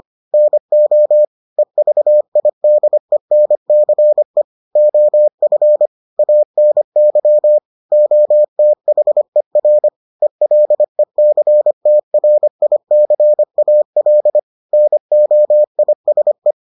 Morse code, heard as a keyed tone, about 25 wpm, 600 Hz. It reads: NO EVIDENCE OF ANY OTHER ELECTRICAL NOISE